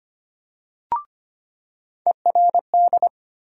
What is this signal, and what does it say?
Morse code, heard as a keyed tone, about 25 wpm, 700 Hz. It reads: ERD
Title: USA Call Sign Postfixes - 3 Letters 25wpm